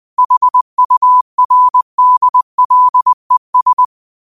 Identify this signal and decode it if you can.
Morse code, heard as a keyed tone, about 20 wpm, 1000 Hz. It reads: HURDLES
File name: qotdfiles / 20wpm